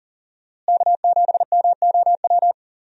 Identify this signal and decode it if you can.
Morse code, heard as a keyed tone, about 40 wpm, 700 Hz. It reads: K7MOW
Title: USA Call Signs - 5 Characters - Repeated 2x - 40wpm